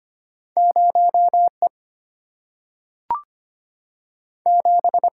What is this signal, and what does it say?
Morse code, heard as a keyed tone, about 25 wpm, 700 Hz. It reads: 0 E 7
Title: Single-Letter Single-Number 25wpm